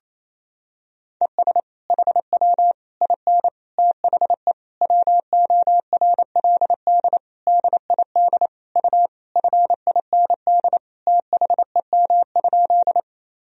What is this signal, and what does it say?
Morse code, heard as a keyed tone, about 28 wpm, 700 Hz. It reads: ES HW IN THE WORLD DID U FIND THEM?
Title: Sentences from Top 200 Words and Abbreviations 28wpm